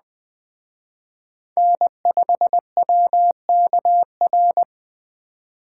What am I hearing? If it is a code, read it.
Morse code, heard as a keyed tone, about 20 wpm, 700 Hz. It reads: N5WKR
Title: Speed Racing - USA 1x3 Call Signs 17wpm